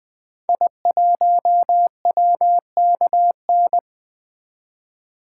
Morse code, heard as a keyed tone, about 20 wpm, 700 Hz. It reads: I1WKN
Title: 2023 SOTA Activator Call Signs - 20wpm